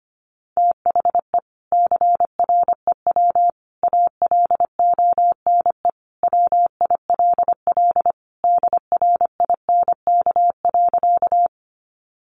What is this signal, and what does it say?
Morse code, heard as a keyed tone, about 25 wpm, 700 Hz. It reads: THE CREW ALONE WILL DRINK.
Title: Sentences from Top 1500 Words 25wpm